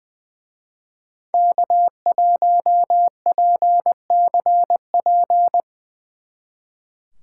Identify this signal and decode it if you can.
Morse code, heard as a keyed tone, about 23 wpm, 700 Hz. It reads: K1PCP
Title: Speed Racing - USA Call Signs 20wpm